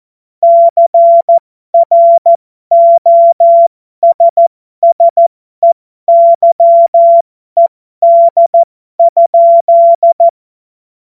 Morse code, heard as a keyed tone, about 14 wpm, spaced as for 12 wpm, 680 Hz. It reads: CROSSEYED?
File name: Morsekurs / Witze